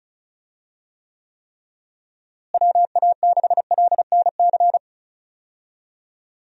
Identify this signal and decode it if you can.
Morse code, heard as a keyed tone, about 34 wpm, 700 Hz. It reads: WA6LNC
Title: USA Call Signs - 6 Characters - Repeated 3x - 35wpm